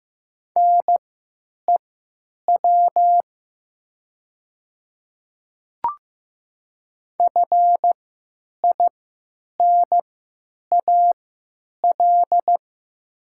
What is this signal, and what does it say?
Morse code, heard as a keyed tone, about 15 wpm, spaced as for 9 wpm, 700 Hz. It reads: NEW FINAL
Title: 3x Character Spacing - Top 500 Words - 15wpm